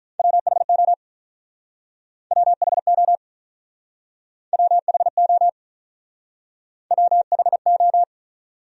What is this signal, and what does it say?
Morse code, heard as a keyed tone, about 46 wpm, 700 Hz. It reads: WHO WHO WHO WHO